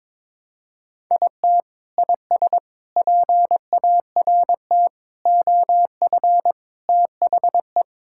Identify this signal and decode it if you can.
Morse code, heard as a keyed tone, about 21 wpm, 700 Hz. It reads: IT IS PART OF THE
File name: practice / 22wpm / Sentences with 4-Word Phrases - 22wpm